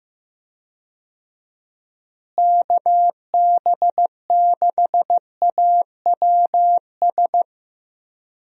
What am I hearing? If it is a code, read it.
Morse code, heard as a keyed tone, about 15 wpm, 700 Hz. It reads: KB6AWS
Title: USA Call Signs - 6 Characters - Repeated 3x - 15wpm